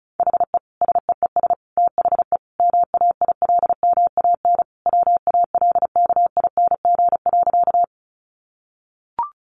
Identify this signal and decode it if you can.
Morse code, heard as a keyed tone, about 34 wpm, 700 Hz. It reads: HE SEES THE MAILMAN WALKING.
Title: Easy Phrases 35wpm